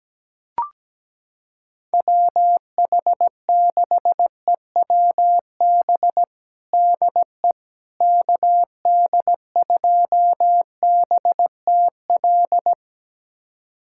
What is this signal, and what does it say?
Morse code, heard as a keyed tone, about 17 wpm, 700 Hz. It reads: WH6EWB DE KD2BTL
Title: USA Callsign DE Callsign - 6 Characters - 17wpm